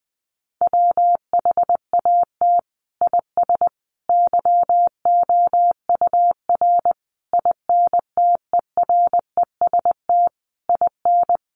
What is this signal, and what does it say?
Morse code, heard as a keyed tone, about 20 wpm, 700 Hz. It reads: WHAT IS YOUR INTEREST IN
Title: Sentences from Top 1300 Words 20wpm